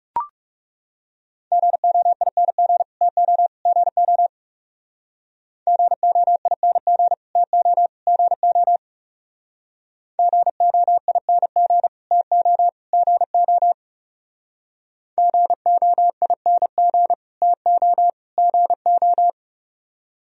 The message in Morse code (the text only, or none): GOING TO GO GOING TO GO GOING TO GO GOING TO GO